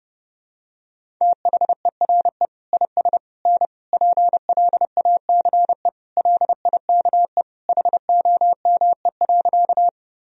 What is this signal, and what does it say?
Morse code, heard as a keyed tone, about 30 wpm, 700 Hz. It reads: THERE IS N PLACE LIKE HOME.
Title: Sentences from Top 200 Words and Abbreviations 30wpm